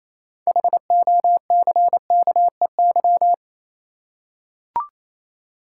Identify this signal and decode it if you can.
Morse code, heard as a keyed tone, about 27 wpm, 690 Hz. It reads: HOCKEY
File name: Easy Words - Speed-Racing with Hints - 22wpm